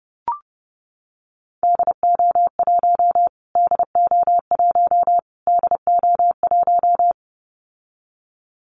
Morse code, heard as a keyed tone, about 30 wpm, 700 Hz. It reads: DO1 DO1 DO1
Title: Callsign Prefixes for Top 25 DXCC - Repeated 3x - 30wpm